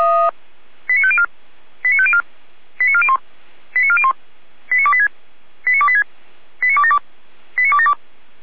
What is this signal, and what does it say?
Five-tone selective call (ZVEI), heard as quick runs of five tones, tones of about 70 ms. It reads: 89504 89504 89401 89401 89286 89286 89272 89272
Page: /455/SDR/ZVEI/89270/